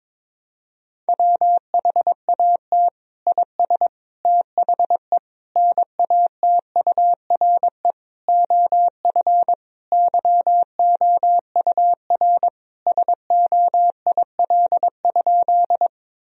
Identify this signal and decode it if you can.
Morse code, heard as a keyed tone, about 22 wpm, 700 Hz. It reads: WHAT IS THE NATURE OF YOUR SOIL?
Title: Sentences from Top 1000 Words - Review Part 4 - 22wpm